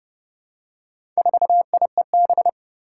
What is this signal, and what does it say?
Morse code, heard as a keyed tone, about 30 wpm, 700 Hz. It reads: 4IEB
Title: Four Challenging Letter-Number Combinations 30wpm